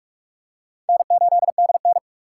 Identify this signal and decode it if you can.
Morse code, heard as a keyed tone, about 42 wpm, 700 Hz. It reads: N8DN